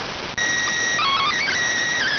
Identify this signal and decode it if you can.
SSTV (slow-scan television), PD 290.